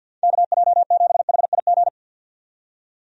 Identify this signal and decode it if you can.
Morse code, heard as a keyed tone, about 52 wpm, 700 Hz. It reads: KJ7HIG
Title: USA Call Signs - 6 Characters - 50wpm